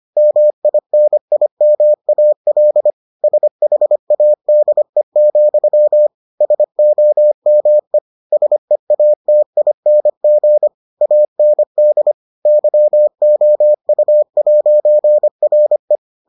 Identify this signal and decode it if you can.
Morse code, heard as a keyed tone, about 25 wpm, 600 Hz. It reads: MINIMAL SHADE, SOME SEATING AND YOU'RE